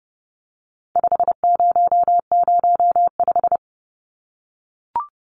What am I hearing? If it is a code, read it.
Morse code, heard as a keyed tone, about 30 wpm, 700 Hz. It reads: 5005
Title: Numbers - 4-Digit 30wpm